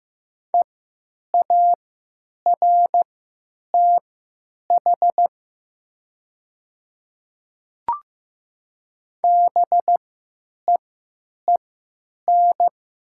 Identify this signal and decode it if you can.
Morse code, heard as a keyed tone, about 15 wpm, spaced as for 9 wpm, 700 Hz. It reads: EARTH BEEN
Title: 3x Character Spacing - Top 300 Words - 15wpm